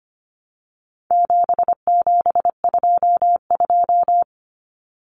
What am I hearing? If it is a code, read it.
Morse code, heard as a keyed tone, about 25 wpm, 700 Hz. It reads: 7722
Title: Numbers - 4-Digit 25wpm